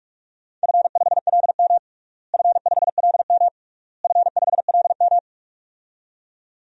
Morse code, heard as a keyed tone, about 46 wpm, 700 Hz. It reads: F5LM F5LM F5LM
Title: DXCC-100 Call Signs - 4 Characters - Repeated 3x - 45wpm